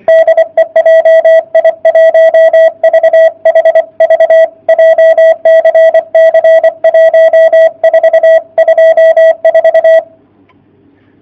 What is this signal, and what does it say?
Morse code, heard as a keyed tone, about 23 wpm, 660 Hz. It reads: DE JI1VHV JCC 1424
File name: jh8cbh / 2019cw / sp31- w